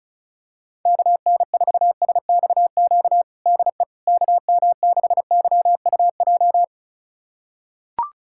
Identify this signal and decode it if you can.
Morse code, heard as a keyed tone, about 36 wpm, 700 Hz. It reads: KN4SXQ DE KM6YUJ
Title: USA Callsign DE Callsign 35wpm